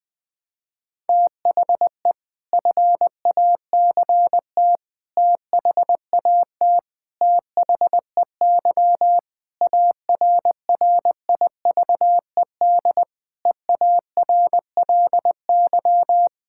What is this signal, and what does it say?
Morse code, heard as a keyed tone, about 20 wpm, 700 Hz. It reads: THE FACT THAT THEY ARRIVED EARLY
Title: Sentences with 4-Word Phrases - 20wpm